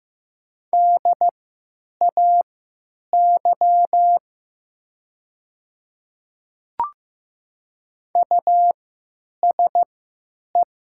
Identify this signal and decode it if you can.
Morse code, heard as a keyed tone, about 15 wpm, spaced as for 8 wpm, 700 Hz. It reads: DAY USE
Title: 3x Character Spacing - Top 200 Words - 15wpm